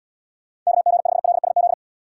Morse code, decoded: KG5LSZ

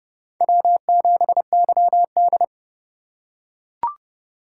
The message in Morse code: W7YD

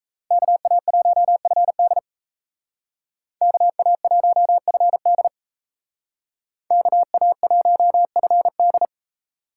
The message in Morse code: KA1FD KA1FD KA1FD

